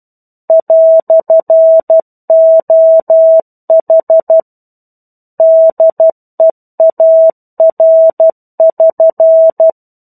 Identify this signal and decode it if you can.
Morse code, heard as a keyed tone, about 12 wpm, 650 Hz. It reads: "OH DEAR<SN>
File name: 01-Chapter-01 12 WPM